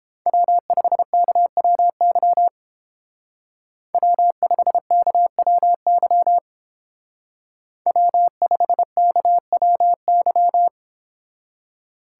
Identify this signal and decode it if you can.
Morse code, heard as a keyed tone, about 30 wpm, 700 Hz. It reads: W5KWY W5KWY W5KWY